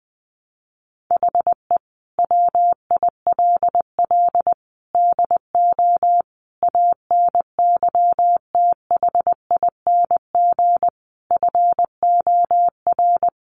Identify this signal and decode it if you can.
Morse code, heard as a keyed tone, about 20 wpm, 700 Hz. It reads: HE WILL DO ANYTHING FOR
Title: Sentences from Top 1000 Words 20wpm